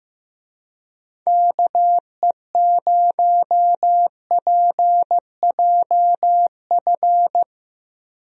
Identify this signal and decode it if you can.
Morse code, heard as a keyed tone, about 15 wpm, 700 Hz. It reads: KE0PJF
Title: USA Call Signs - 6 Characters - Repeated 3x - 15wpm